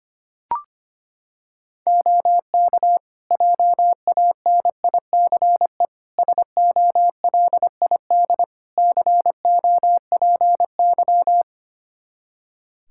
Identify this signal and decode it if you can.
Morse code, heard as a keyed tone, about 25 wpm, 700 Hz. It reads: OK JANICE SOLID COPY
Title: Practice QSO - Exchange 3 - More Info - 25wpm